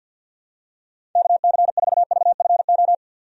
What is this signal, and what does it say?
Morse code, heard as a keyed tone, about 51 wpm, 700 Hz. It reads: KC4VFO
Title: USA Call Signs - 6 Characters - 50wpm